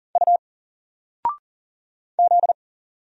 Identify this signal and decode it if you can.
Morse code, heard as a keyed tone, about 40 wpm, 700 Hz. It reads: U Z
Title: Single Character (Letter:Number:Punctuation:Prosign) 40wpm